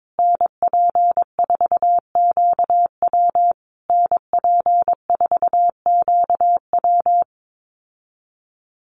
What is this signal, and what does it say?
Morse code, heard as a keyed tone, about 22 wpm, 700 Hz. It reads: NP4QW NP4QW
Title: USA Call Signs Repeated 3x 22wpm